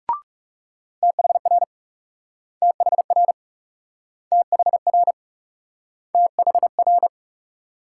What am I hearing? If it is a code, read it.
Morse code, heard as a keyed tone, about 38 wpm, 700 Hz. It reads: THR THR THR THR